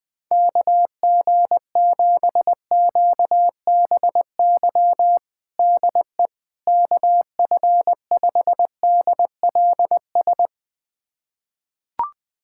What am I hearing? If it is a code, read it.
Morse code, heard as a keyed tone, about 20 wpm, 700 Hz. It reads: KG7QBY DE KF5DLS
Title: USA Callsign DE Callsign - 6 Characters - 20wpm